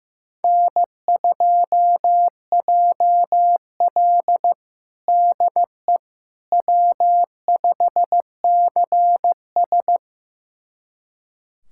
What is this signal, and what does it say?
Morse code, heard as a keyed tone, about 15 wpm, 700 Hz. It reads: N2JL DE W5CS
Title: USA Callsign DE Callsign - 4 Characters - 15wpm